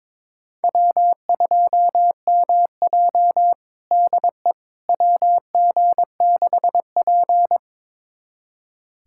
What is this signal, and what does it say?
Morse code, heard as a keyed tone, about 22 wpm, 700 Hz. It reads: W2MJ DE WG6P
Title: USA Callsign DE Callsign - 4 Characters - 22wpm